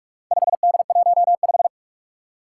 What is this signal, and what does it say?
Morse code, heard as a keyed tone, about 44 wpm, 700 Hz. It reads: 5D15
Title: Four Challenging Letter-Number Combinations 45wpm